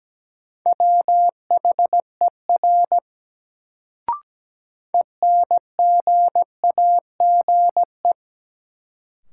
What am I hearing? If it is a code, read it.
Morse code, heard as a keyed tone, about 17 wpm, 700 Hz. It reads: WHER ENGAGE
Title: Sentences from Top 1500 Words - New Word Drill Alt - 17wpm